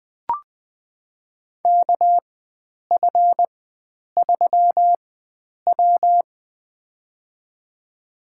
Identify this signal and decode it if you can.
Morse code, heard as a keyed tone, about 20 wpm, spaced as for 9 wpm, 700 Hz. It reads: KF3W
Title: USA Call Signs - 4 Characters - 4x Character Spacing - 20wpm